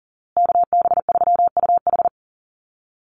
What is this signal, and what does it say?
Morse code, heard as a keyed tone, about 39 wpm, 700 Hz. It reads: KB3UH
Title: USA Call Signs Repeated 3x 40wpm